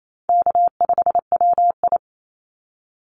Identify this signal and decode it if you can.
Morse code, heard as a keyed tone, about 28 wpm, 700 Hz. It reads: K5WI
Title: NA SOTA Chaser Callsigns 28wpm